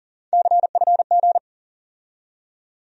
Morse code, heard as a keyed tone, about 38 wpm, 700 Hz. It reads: CFG
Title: USA Call Sign Postfixes - 3 Letters 40wpm